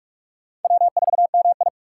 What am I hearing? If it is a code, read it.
Morse code, heard as a keyed tone, about 44 wpm, 700 Hz. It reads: W4MI
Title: USA Call Signs - 2x Character Spacing - 45wpm